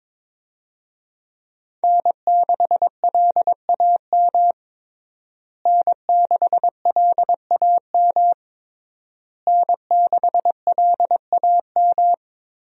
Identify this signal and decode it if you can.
Morse code, heard as a keyed tone, about 21 wpm, 700 Hz. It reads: N6LAM N6LAM N6LAM
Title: USA Call Signs - 5 Characters - Repeated 3x - 22wpm